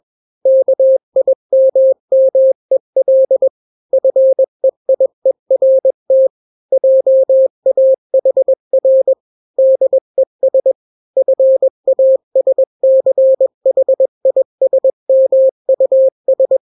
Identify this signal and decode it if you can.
Morse code, heard as a keyed tone, about 21 wpm, 530 Hz. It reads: KIMMEL FEIERT JAHR DES FASCHISMUS
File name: daily / 20wpm